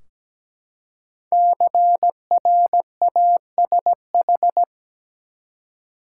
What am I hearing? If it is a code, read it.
Morse code, heard as a keyed tone, about 17 wpm, 700 Hz. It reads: CRASH
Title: Five Letter Words - 17wpm